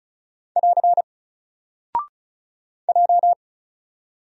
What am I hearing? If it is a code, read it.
Morse code, heard as a keyed tone, about 35 wpm, 700 Hz. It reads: <AR> J
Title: Single Character (Letter:Number:Punctuation:Prosign) 35wpm